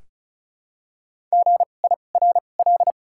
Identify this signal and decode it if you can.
Morse code, heard as a keyed tone, about 36 wpm, spaced as for 25 wpm, 700 Hz. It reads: GIRL
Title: 2x Character Spacing - Top 500 Words - 35wpm